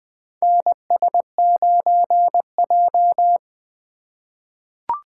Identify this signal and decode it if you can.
Morse code, heard as a keyed tone, about 21 wpm, 700 Hz. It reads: NS9J
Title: USA Call Signs - 6x Character Spacing - 20wpm